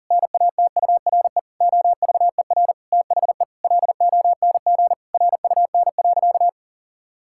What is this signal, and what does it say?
Morse code, heard as a keyed tone, about 40 wpm, 700 Hz. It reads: NATURE OVER THE LONG RUN.
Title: Sentences from Top 1000 Words - Review Part 6 - 40wpm